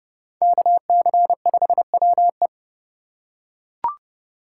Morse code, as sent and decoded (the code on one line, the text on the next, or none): -.- -.-. ..... .-- .
KC5WE